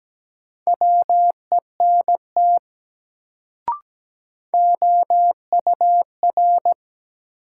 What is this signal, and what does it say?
Morse code, heard as a keyed tone, about 17 wpm, 700 Hz. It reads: WENT OUR
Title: Sentences from Top 200 Words - New Word Drill - 17wpm